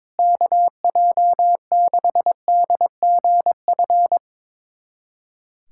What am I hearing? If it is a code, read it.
Morse code, heard as a keyed tone, about 22 wpm, 700 Hz. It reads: KJ6DGF
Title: USA Callsign DE Callsign - 6 Characters - 22wpm